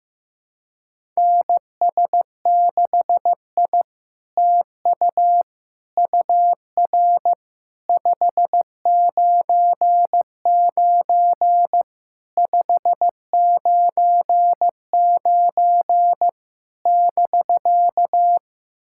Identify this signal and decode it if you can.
Morse code, heard as a keyed tone, about 15 wpm, 700 Hz. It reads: NS6I TU UR 599 599 <BK>
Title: SOTA QSO Practice 15wpm